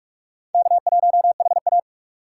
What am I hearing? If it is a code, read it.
Morse code, heard as a keyed tone, about 46 wpm, 700 Hz. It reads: K1HA
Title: USA Call Signs - 4 Characters - 3x Character Spacing - 45wpm